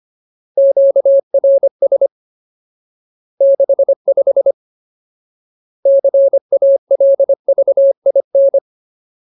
Morse code, heard as a keyed tone, about 25 wpm, 550 Hz. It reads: QRS 65 CALVIN
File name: Ultimate Word List / 25WPM Ultimate